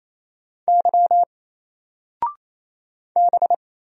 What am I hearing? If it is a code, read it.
Morse code, heard as a keyed tone, about 28 wpm, 700 Hz. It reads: Y B
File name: Single-Letter-Number-28wpm